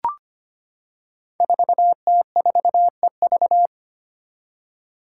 Morse code, heard as a keyed tone, about 25 wpm, 700 Hz. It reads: 4T4EV
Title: Five Challenging Letter-Number Combinations 25wpm